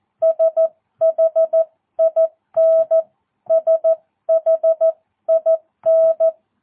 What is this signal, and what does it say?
Morse code, heard as a keyed tone, about 14 wpm, 650 Hz. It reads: SHIN SHIN
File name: jh8cbh / 2022cw